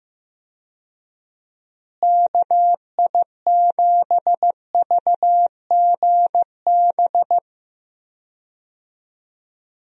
Morse code, heard as a keyed tone, about 15 wpm, 700 Hz. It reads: KI7VGB
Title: USA Call Signs - 6 Characters - Repeated 2x - 15wpm